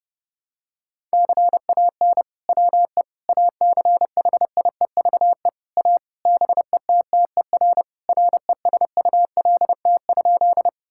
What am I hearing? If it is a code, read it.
Morse code, heard as a keyed tone, about 30 wpm, 700 Hz. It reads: CAN WE ACHIEVE A BETTER RESULT?
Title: Sentences from Top 1000 Words - Review Part 6 - 30wpm